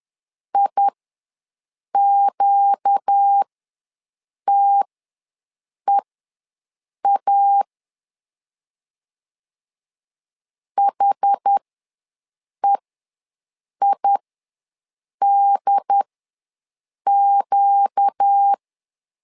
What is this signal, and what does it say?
Morse code, heard as a keyed tone, about 10 wpm, spaced as for 6 wpm, 800 Hz. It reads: IQTEA HEIDQ
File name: LEZIONI / 4 Q ws 10wpm 2306